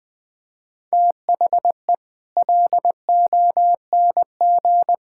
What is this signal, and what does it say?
Morse code, heard as a keyed tone, about 21 wpm, 700 Hz. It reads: THE LONG
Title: Sets of 8 Words - 20wpm